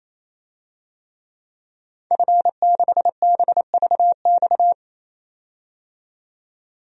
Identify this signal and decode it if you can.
Morse code, heard as a keyed tone, about 28 wpm, 700 Hz. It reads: F6BVX
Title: DXCC-100 Call Signs - 5 Characters - 28wpm